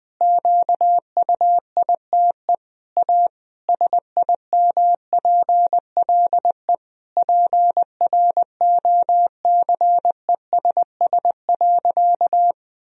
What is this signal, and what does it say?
Morse code, heard as a keyed tone, about 21 wpm, 700 Hz. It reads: QUITE A SIMPLE PROCESS.
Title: Sentences from Top 1000 Words - Review Part 3 - 20wpm